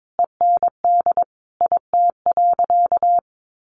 Morse code, heard as a keyed tone, about 22 wpm, 700 Hz. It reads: END IT.
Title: Sentences from Top 1200 Words 22wpm